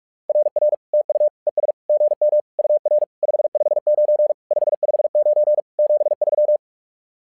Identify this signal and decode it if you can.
Morse code, heard as a keyed tone, about 44 wpm, 600 Hz. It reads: RR TU ES GM UR 559 559 73
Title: SOTA QSO Practice 45wpm